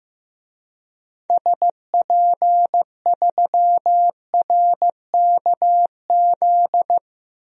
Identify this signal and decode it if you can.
Morse code, heard as a keyed tone, about 15 wpm, 700 Hz. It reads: SP3RKZ
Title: DXCC-100 Call Signs - 6 Characters - 15wpm